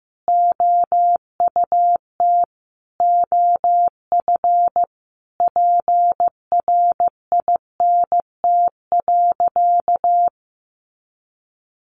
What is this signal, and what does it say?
Morse code, heard as a keyed tone, about 15 wpm, 700 Hz. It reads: OUT OF PRINT.
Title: Sentences from Top 1400 Words 15wpm